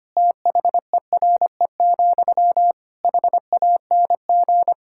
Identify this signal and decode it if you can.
Morse code, heard as a keyed tone, about 25 wpm, 700 Hz. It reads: THERE, HANG
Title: Sentences from Top 1600 Words 25wpm